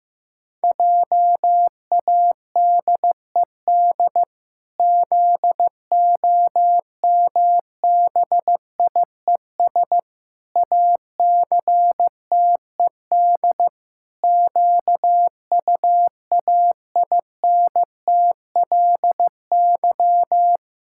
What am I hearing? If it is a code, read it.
Morse code, heard as a keyed tone, about 15 wpm, 700 Hz. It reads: JADED ZOMBIES ACTED QUAINTLY
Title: Pangram Challenge v2 - 15wpm